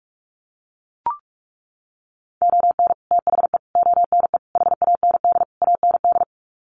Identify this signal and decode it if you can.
Morse code, heard as a keyed tone, about 44 wpm, 700 Hz. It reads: ON THE ONE HAND AND
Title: Sets of 5 Words Encore2 45wpm